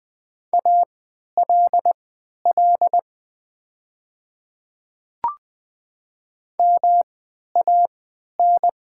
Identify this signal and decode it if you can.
Morse code, heard as a keyed tone, about 20 wpm, spaced as for 11 wpm, 700 Hz. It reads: ALL MAN
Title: 3x Character Spacing - Top 500 Words - 20wpm